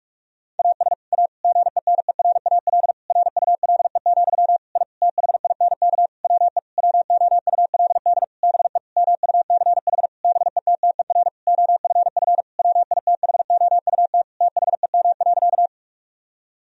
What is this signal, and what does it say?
Morse code, heard as a keyed tone, about 46 wpm, 700 Hz. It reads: AS A GENERAL RULE, I THINK WE WOULD BE MUCH BETTER OFF WITHOUT THEM.
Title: Sentences from Top 1000 Words - Review Part 3 - 45wpm